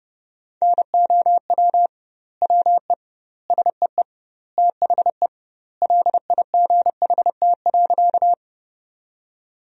Morse code, heard as a keyed tone, about 30 wpm, 700 Hz. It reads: NOW WE SEE THE LIGHT.
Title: 2x Word-Spacing - Sentences from Top 200 Words - 30wpm